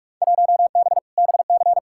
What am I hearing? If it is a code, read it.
Morse code, heard as a keyed tone, about 46 wpm, 700 Hz. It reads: 1B BC
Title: Field Day Exchanges - 45wpm